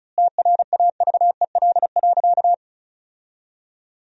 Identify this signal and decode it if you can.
Morse code, heard as a keyed tone, about 36 wpm, 700 Hz. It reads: TRAVEL.
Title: 3x Word-Spacing - Sentences from Top 400 Words - 35wpm